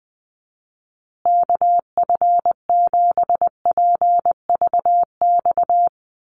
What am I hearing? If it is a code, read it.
Morse code, heard as a keyed tone, about 20 wpm, 700 Hz. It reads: KF7PVX
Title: USA Call Signs Repeated 3x 20wpm